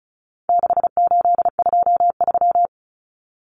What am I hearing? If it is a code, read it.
Morse code, heard as a keyed tone, about 34 wpm, 700 Hz. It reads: 6823
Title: Numbers - 4-Digit 35wpm